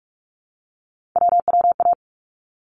Morse code, heard as a keyed tone, about 46 wpm, 700 Hz. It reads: WWA
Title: ARRL Sections 45wpm